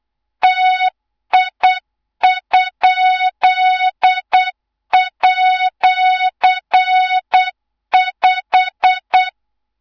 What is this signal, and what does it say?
Morse code, heard as a keyed tone, about 8 wpm, 3700 Hz, with noise. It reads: TI?@5
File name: Morze-HA2MN / hang / 24 GYAKORLAT